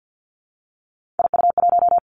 Morse code, heard as a keyed tone, about 50 wpm, 700 Hz. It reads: IU1
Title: Three Challenging Letter-Number Combinations 50wpm